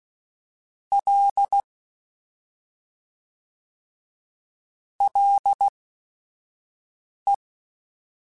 Morse code, heard as a keyed tone, about 16 wpm, spaced as for 5 wpm, 800 Hz. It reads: L LE
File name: Ls / Single L m3s / Single L @5